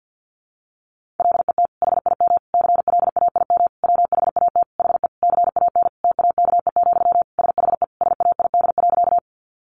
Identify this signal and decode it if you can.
Morse code, heard as a keyed tone, about 50 wpm, 700 Hz. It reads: LET HIM CLAIM WHAT HE CAN TAKE, SHE SAID.
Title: Sentences from Top 1500 Words 50wpm